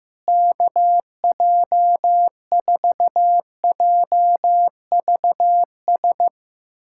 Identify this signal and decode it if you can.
Morse code, heard as a keyed tone, about 15 wpm, 700 Hz. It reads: KJ4JVS
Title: USA Callsign DE Callsign 15wpm